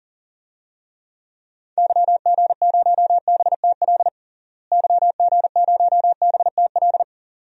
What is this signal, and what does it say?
Morse code, heard as a keyed tone, about 40 wpm, 700 Hz. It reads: YG0BTL YG0BTL
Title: DXCC-100 Call Signs - 6 Characters - Repeated 2x - 40wpm